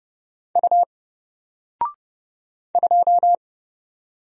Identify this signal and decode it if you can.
Morse code, heard as a keyed tone, about 30 wpm, 700 Hz. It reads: U 2
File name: Single-Letter-Number-30wpm